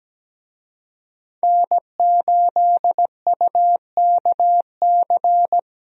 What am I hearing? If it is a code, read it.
Morse code, heard as a keyed tone, about 17 wpm, 700 Hz. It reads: N8UKC
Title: USA Call Signs - 5 Characters - 17wpm